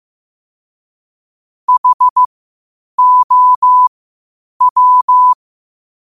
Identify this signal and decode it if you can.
Morse code, heard as a keyed tone, about 15 wpm, spaced as for 8 wpm, 1000 Hz. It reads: HOW
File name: qotdfiles / 8wpm